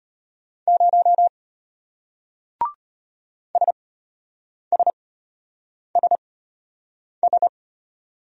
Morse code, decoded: 0 SSSS

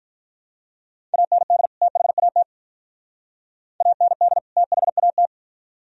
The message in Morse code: AND THAT AND THAT

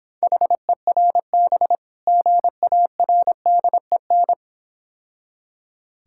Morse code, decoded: HERB GARDEN

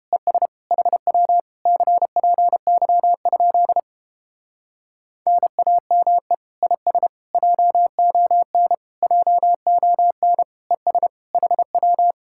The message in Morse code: ES HW CPY? NAME IS JON JON ES HW

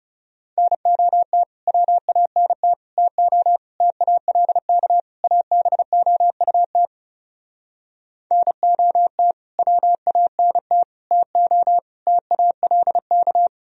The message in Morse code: NOT WANT TO TALK ABOUT NOT WANT TO TALK

